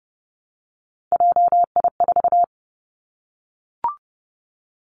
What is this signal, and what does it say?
Morse code, heard as a keyed tone, about 30 wpm, 700 Hz. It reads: JI4
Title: Three Challenging Letter-Number Combinations 30wpm